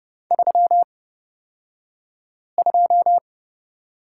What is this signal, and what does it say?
Morse code, heard as a keyed tone, about 30 wpm, spaced as for 5 wpm, 700 Hz. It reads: 32
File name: Lesson 38d - Letter q - Two Characters 30wpm